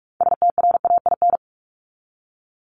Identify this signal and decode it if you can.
Morse code, heard as a keyed tone, about 44 wpm, 700 Hz. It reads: STRAIN
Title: Top 1751-2000 Words 45wpm